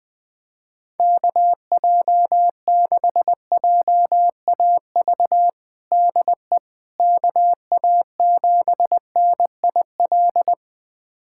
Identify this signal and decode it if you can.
Morse code, heard as a keyed tone, about 20 wpm, 700 Hz. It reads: KJ6JAV DE KA7NIL
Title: USA Callsign DE Callsign 20wpm